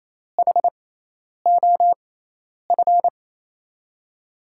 Morse code, heard as a keyed tone, about 27 wpm, spaced as for 10 wpm, 700 Hz. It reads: HOF
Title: USA Call Signs - 5 Characters - 6x Character Spacing - 28wpm